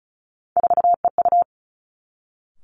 Morse code, heard as a keyed tone, about 36 wpm, 700 Hz. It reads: 4EU
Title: Three Challenging Letter-Number Combinations 35wpm